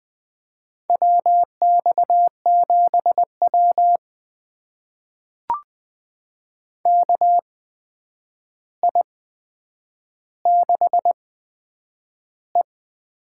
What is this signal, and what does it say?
Morse code, heard as a keyed tone, about 20 wpm, spaced as for 9 wpm, 700 Hz. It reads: WX7W KI6E
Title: USA Call Signs - 4 Characters - 8x Character Spacing - 20wpm